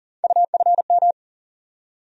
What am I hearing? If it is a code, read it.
Morse code, heard as a keyed tone, about 38 wpm, 700 Hz. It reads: UFM
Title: USA Call Sign Postfixes - 3 Letters 40wpm